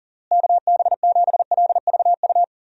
Morse code, heard as a keyed tone, about 37 wpm, 700 Hz. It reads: KB7LVU
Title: USA Call Signs - 6 Characters - Repeated 3x - 40wpm